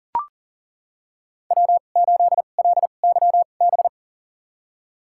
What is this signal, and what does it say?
Morse code, heard as a keyed tone, about 40 wpm, spaced as for 29 wpm, 700 Hz. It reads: W8LYB
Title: USA Call Signs - 2x Character Spacing - 40wpm